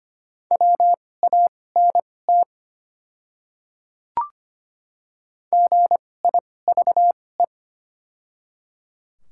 Morse code, heard as a keyed tone, about 25 wpm, spaced as for 18 wpm, 700 Hz. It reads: WANT GIVE
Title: 2x Character Spacing - Top 200 Words - 25wpm